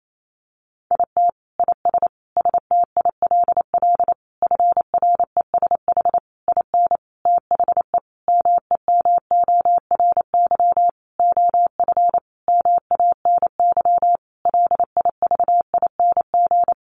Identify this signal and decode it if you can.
Morse code, heard as a keyed tone, about 28 wpm, 700 Hz. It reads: IT IS STILL FRESH IN THE MEMORY OF MANY LIVING
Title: Sentences from Top 1300 Words 28wpm